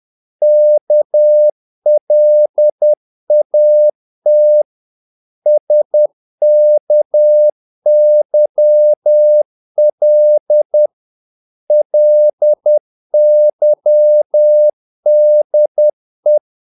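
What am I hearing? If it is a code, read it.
Morse code, heard as a keyed tone, about 10 wpm, 600 Hz. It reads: KLAT SKYL LYDE
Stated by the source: Morse practice of short Danish words at 10 wpm, from CW med Gnister.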